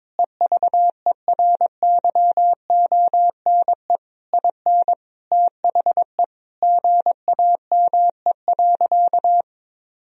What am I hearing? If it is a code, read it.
Morse code, heard as a keyed tone, about 22 wpm, 700 Hz. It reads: EVERYONE IN THE GAME.
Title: Sentences from Top 1000 Words - Review Part 4 - 22wpm